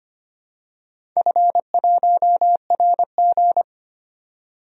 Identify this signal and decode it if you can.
Morse code, heard as a keyed tone, about 25 wpm, 700 Hz. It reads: F1RG
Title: DXCC-100 Call Signs - 4 Characters - 25wpm